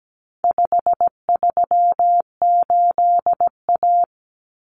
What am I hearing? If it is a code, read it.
Morse code, heard as a keyed tone, about 17 wpm, 700 Hz. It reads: 538A
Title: Common Cut Numbers - 4-Digit 17wpm